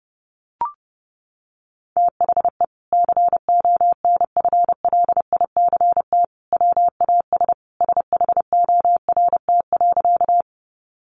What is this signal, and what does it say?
Morse code, heard as a keyed tone, about 30 wpm, 700 Hz. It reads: THE CONFLICT WAS SHORT.
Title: Sentences from Top 1300 Words 30wpm